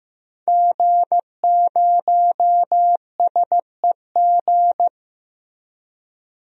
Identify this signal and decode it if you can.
Morse code, heard as a keyed tone, about 15 wpm, 700 Hz. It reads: G0SEG
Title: DXCC-100 Call Signs - 5 Characters - Repeated 3x - 15wpm